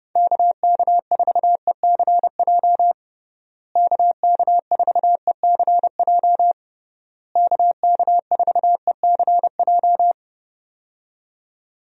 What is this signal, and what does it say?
Morse code, heard as a keyed tone, about 32 wpm, 700 Hz. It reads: KK4ECJ KK4ECJ KK4ECJ
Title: USA Call Signs - 6 Characters - Repeated 3x - 30wpm